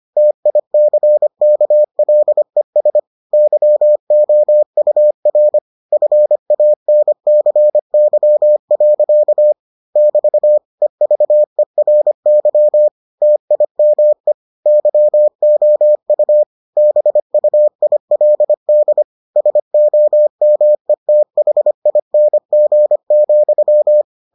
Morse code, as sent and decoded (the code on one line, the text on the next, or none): - .. -.-. -.- .-.. ... / -.-- --- ..- .-. / ..-. .- -. -.-. -.-- .-.-.- / -...- / . ...- . .-. -.-- / - .. -- . / -.-- --- ..- / -... ..- .. .-.. -.. / ... --- -- . - .... .. -. --. --..--
TICKLES YOUR FANCY. <BT> EVERY TIME YOU BUILD SOMETHING,